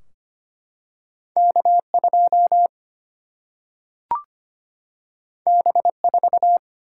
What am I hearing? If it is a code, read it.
Morse code, heard as a keyed tone, about 25 wpm, 700 Hz. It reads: K2 B4
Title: Single-Letter Single-Number 25wpm